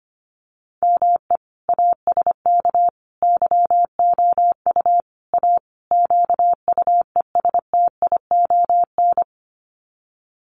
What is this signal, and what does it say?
Morse code, heard as a keyed tone, about 25 wpm, 700 Hz. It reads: ME ASK YOU A QUESTION
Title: Sets of 5 Words Encore2 25wpm